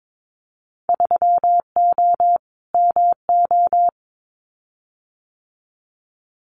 Morse code, heard as a keyed tone, about 22 wpm, 700 Hz. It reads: 3O MO
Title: Winter Field Day Exchanges Encore 22wpm